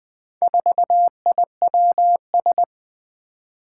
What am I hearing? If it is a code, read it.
Morse code, heard as a keyed tone, about 20 wpm, 700 Hz. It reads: 4IWS
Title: Four Challenging Letter-Number Combinations 20wpm